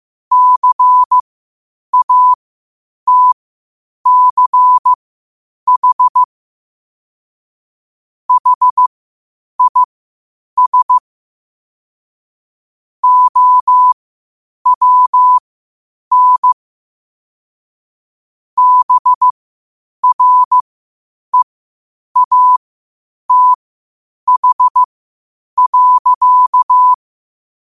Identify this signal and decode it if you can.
Morse code, heard as a keyed tone, about 15 wpm, spaced as for 8 wpm, 1000 Hz. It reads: CATCH HIS OWN BREATH.
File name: qotdfiles / 8wpm